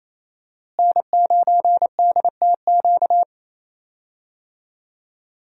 Morse code, heard as a keyed tone, about 28 wpm, 700 Hz. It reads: N9DTQ
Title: USA Call Signs - 5 Characters - Repeated 3x - 28wpm